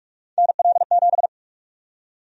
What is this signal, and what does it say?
Morse code, heard as a keyed tone, about 42 wpm, 700 Hz. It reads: NL7
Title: USA Call Sign Prefixes - 45wpm